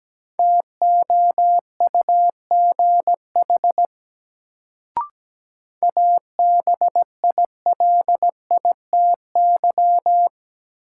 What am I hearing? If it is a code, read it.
Morse code, heard as a keyed tone, about 17 wpm, 700 Hz. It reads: TOUGH ABILITY
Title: Sentences from Top 1200 Words - New Word Drill Alt - 17wpm